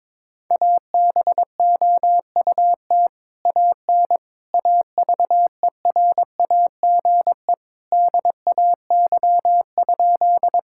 Morse code, heard as a keyed tone, about 22 wpm, 700 Hz. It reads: ABOUT AN AVERAGE DAY?
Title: Sentences from Top 1000 Words - Review Part 6 - 22wpm